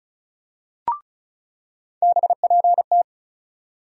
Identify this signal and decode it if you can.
Morse code, heard as a keyed tone, about 36 wpm, 700 Hz. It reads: BPT
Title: USA Call Sign Postfixes - 3 Letters 35wpm